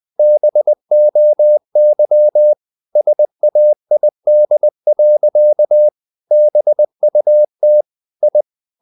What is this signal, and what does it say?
Morse code, heard as a keyed tone, about 20 wpm, 600 Hz. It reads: BOY SAID. BUT I
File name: the old man and the sea / the old man and the sea part 1 20wpm